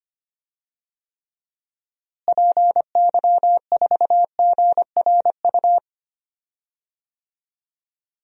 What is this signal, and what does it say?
Morse code, heard as a keyed tone, about 25 wpm, 700 Hz. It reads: PY4GRU
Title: DXCC-100 Call Signs - 6 Characters - Repeated 2x - 25wpm